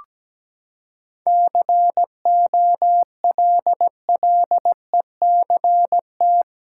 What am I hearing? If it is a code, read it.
Morse code, heard as a keyed tone, about 17 wpm, 700 Hz. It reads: COLLECT
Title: Sentences from Top 1600 Words 17wpm